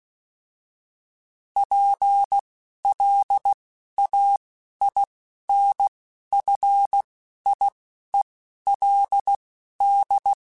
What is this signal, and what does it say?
Morse code, heard as a keyed tone, about 16 wpm, spaced as for 11 wpm, 800 Hz. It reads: PLAINFIELD